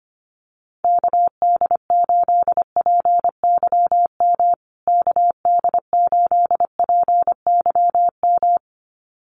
Morse code, heard as a keyed tone, about 25 wpm, 700 Hz. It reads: KD8PYM KD8PYM
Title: USA Call Signs Repeated 2x 25wpm